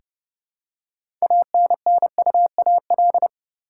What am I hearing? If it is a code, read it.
Morse code, heard as a keyed tone, about 30 wpm, 700 Hz. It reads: ANNUAL
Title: Six Letter Words - 30wpm